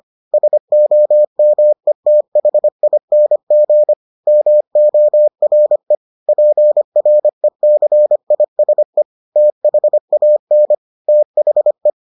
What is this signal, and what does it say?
Morse code, heard as a keyed tone, about 25 wpm, 600 Hz. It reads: SOMETHING MORE PRECISE THAN THE